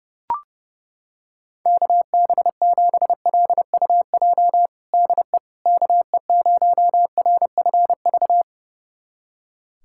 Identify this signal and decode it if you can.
Morse code, heard as a keyed tone, about 30 wpm, 700 Hz. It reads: KB7LUJ DE KE0RFV
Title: USA Callsign DE Callsign - 6 Characters - 30wpm